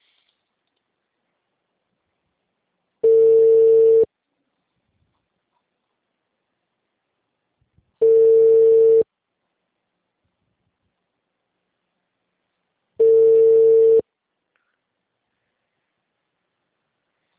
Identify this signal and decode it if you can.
Ringback tone (China, 450 Hz).
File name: out- -2006- - - .6299